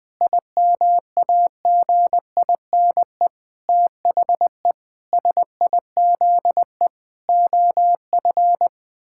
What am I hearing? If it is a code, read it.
Morse code, heard as a keyed tone, about 20 wpm, 700 Hz. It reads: IMAGINE THE SIZE OF
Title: Sentences with 7-Word Phrases - 20wpm